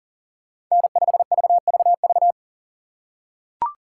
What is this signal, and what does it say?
Morse code, heard as a keyed tone, about 40 wpm, 700 Hz. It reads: N5VVV
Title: USA Call Signs - 5 Characters - 6x Character Spacing - 40wpm